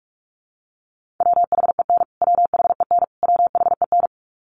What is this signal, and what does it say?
Morse code, heard as a keyed tone, about 44 wpm, 700 Hz. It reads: WHEN WHEN WHEN
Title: Top 200 Words Repeated 45wpm